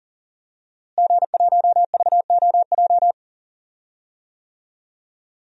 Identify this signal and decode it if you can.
Morse code, heard as a keyed tone, about 40 wpm, 700 Hz. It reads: G1VOJ
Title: DXCC-100 Call Signs - 5 Characters - Repeated 3x - 40wpm